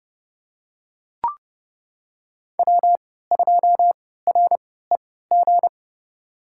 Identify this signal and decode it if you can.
Morse code, heard as a keyed tone, about 30 wpm, spaced as for 17 wpm, 700 Hz. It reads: W2REG
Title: USA Call Signs - 5 Characters - 3x Character Spacing - 30wpm